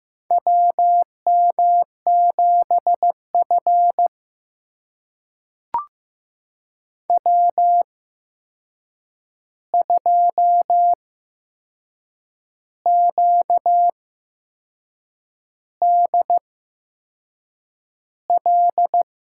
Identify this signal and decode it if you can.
Morse code, heard as a keyed tone, about 15 wpm, spaced as for 6 wpm, 700 Hz. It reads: WM7F W2QDL
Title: USA Call Signs - 8x Character Spacing - 15wpm